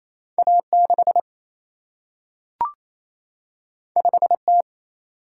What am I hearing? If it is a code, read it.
Morse code, heard as a keyed tone, about 28 wpm, 700 Hz. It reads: A 6 5 T
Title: Single-Letter Single-Number 28wpm